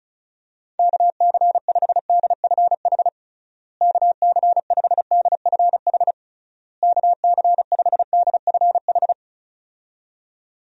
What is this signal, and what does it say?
Morse code, heard as a keyed tone, about 36 wpm, 700 Hz. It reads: KC5DFH KC5DFH KC5DFH
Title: USA Call Signs - 6 Characters - Repeated 3x - 35wpm